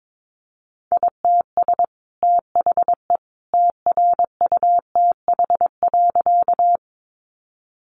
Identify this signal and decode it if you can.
Morse code, heard as a keyed tone, about 22 wpm, 700 Hz. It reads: ITS THE TRUTH.
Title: Sentences from Top 800 Words 22wpm